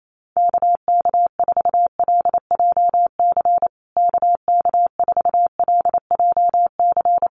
Morse code, heard as a keyed tone, about 28 wpm, 700 Hz. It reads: KK4LJC KK4LJC
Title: USA Call Signs Repeated 3x 28wpm